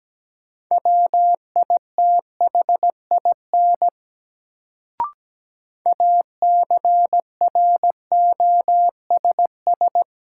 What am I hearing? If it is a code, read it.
Morse code, heard as a keyed tone, about 17 wpm, 700 Hz. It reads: WITHIN ACROSS
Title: Sentences from Top 700 Words - New Word Drill - 17wpm